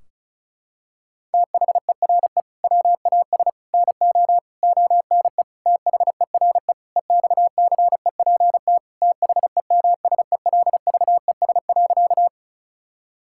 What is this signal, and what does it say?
Morse code, heard as a keyed tone, about 36 wpm, 700 Hz. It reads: THERE WAS NO ONE THERE EXCEPT THEMSELVES.
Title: Sentences from Top 1600 Words 35wpm